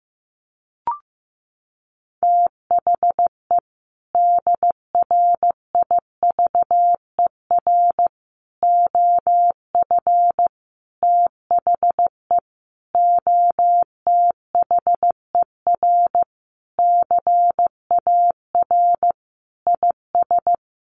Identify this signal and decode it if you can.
Morse code, heard as a keyed tone, about 15 wpm, 700 Hz. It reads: THE DRIVER OF THE OTHER CAR IS
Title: Sentences from Top 1000 Words 15wpm